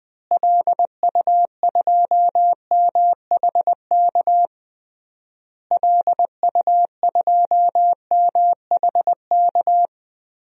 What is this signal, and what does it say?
Morse code, heard as a keyed tone, about 20 wpm, 700 Hz. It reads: LU2MHK LU2MHK
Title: DXCC-100 Call Signs - 6 Characters - Repeated 3x - 20wpm